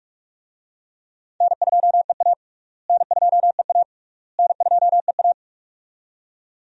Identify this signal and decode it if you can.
Morse code, heard as a keyed tone, about 48 wpm, 700 Hz. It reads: N2EA N2EA N2EA
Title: USA Call Signs - 4 Characters - Repeated 3x - 45wpm